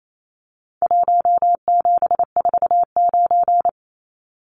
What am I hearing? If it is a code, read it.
Morse code, heard as a keyed tone, about 28 wpm, 700 Hz. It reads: 1749
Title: Numbers - Contest Style - 4-Digit 28wpm